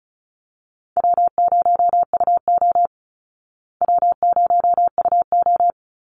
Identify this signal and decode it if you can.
Morse code, heard as a keyed tone, about 34 wpm, 700 Hz. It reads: W0UO W0UO
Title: Call Signs Repeated 2x - 35wpm